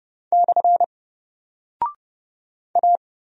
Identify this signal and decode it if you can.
Morse code, heard as a keyed tone, about 30 wpm, 700 Hz. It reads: / A
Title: Single Character (Letter:Number:Punctuation:Prosign) 30wpm